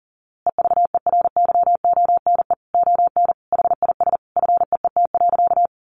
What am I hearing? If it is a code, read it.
Morse code, heard as a keyed tone, about 40 wpm, 700 Hz. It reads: EVERYONE ON HIS FEET.
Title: A Sound Of Thunder 40wpm